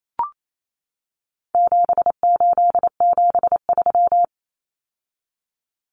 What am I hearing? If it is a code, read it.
Morse code, heard as a keyed tone, about 28 wpm, 700 Hz. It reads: 7873
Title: Common Cut Numbers - 4-Digit 28wpm